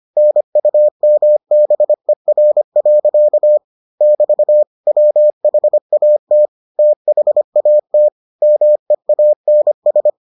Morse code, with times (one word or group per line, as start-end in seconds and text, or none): 0.2-3.6 NUMBER.
4.0-4.6 <BT>
4.9-6.5 WHAT
6.8-8.1 THAT
8.4-10.1 MEANS